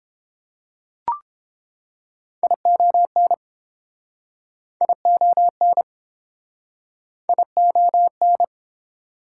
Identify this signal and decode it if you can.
Morse code, heard as a keyed tone, about 29 wpm, 700 Hz. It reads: ION ION ION